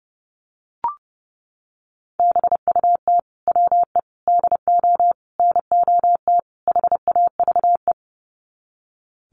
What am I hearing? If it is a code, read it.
Morse code, heard as a keyed tone, about 30 wpm, 700 Hz. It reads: BUT WE DO NOT HAVE
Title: Sets of 5 Words Encore2 30wpm